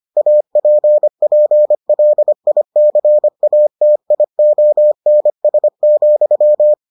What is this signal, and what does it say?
Morse code, heard as a keyed tone, about 25 wpm, 600 Hz. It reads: APPLICATIONS,